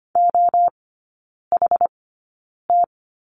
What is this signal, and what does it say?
Morse code, heard as a keyed tone, about 25 wpm, spaced as for 9 wpm, 700 Hz. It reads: OHT
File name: course / 25:10wpm / 093 Lesson 16e - Letter c - Three Characters 25wpm